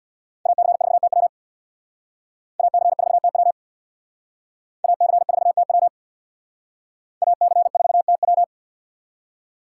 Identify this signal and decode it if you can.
Morse code, heard as a keyed tone, about 60 wpm, 700 Hz. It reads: AC4TW AC4TW AC4TW AC4TW
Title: Speed Racing - USA 2x2 Call Signs 50wpm